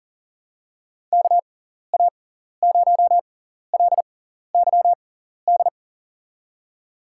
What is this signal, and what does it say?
Morse code, heard as a keyed tone, about 37 wpm, spaced as for 14 wpm, 700 Hz. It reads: KA0LYD
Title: USA Call Signs - 6x Character Spacing - 40wpm